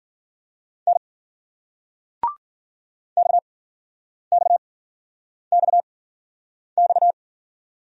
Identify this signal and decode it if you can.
Morse code, heard as a keyed tone, about 50 wpm, spaced as for 9 wpm, 700 Hz. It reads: N XXXX
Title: Speed-Racing Single Letter-Number 40wpm